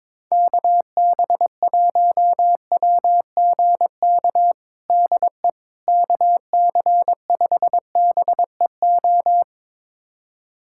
Morse code, heard as a keyed tone, about 22 wpm, 700 Hz. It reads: KB1WGK DE KC5BEO
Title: USA Callsign DE Callsign - 6 Characters - 22wpm